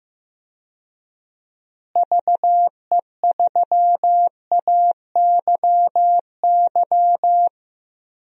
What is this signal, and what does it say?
Morse code, heard as a keyed tone, about 16 wpm, 700 Hz. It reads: VE3AYY
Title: DXCC-100 Call Signs - 6 Characters - Repeated 3x - 15wpm